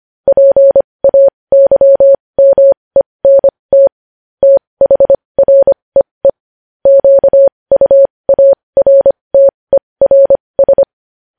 Morse code, heard as a keyed tone, about 25 wpm, spaced as for 20 wpm, 550 Hz. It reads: PAYMENT THREE QUARTERS
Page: news headlines in morse code at 20 wpm